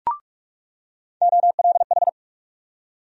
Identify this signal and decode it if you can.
Morse code, heard as a keyed tone, about 46 wpm, 700 Hz. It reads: OLH
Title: USA Call Sign Postfixes - 3 Letters 45wpm